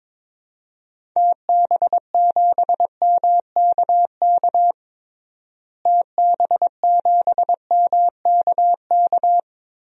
Morse code, heard as a keyed tone, about 22 wpm, 700 Hz. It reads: TB7MKK TB7MKK
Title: DXCC-100 Call Signs - 6 Characters - Repeated 2x - 22wpm